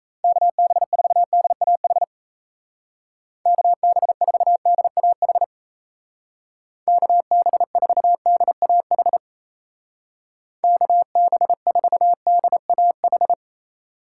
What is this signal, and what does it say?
Morse code, heard as a keyed tone, about 35 wpm, 700 Hz. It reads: KB4DAH KB4DAH KB4DAH KB4DAH
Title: Speed Racing - USA 2x3 Call Signs 28wpm